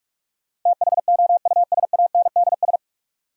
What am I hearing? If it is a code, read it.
Morse code, heard as a keyed tone, about 46 wpm, 700 Hz. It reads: THOUSANDS
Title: Sets of 4 Words - New Word Drill - 45wpm